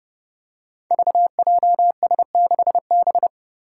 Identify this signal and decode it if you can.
Morse code, heard as a keyed tone, about 31 wpm, 700 Hz. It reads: VJS6B
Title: Five Challenging Letter-Number Combinations 30wpm